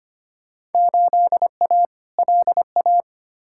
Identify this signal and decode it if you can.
Morse code, heard as a keyed tone, about 25 wpm, 700 Hz. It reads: 8A LA
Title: Field Day Exchanges Encore - 25wpm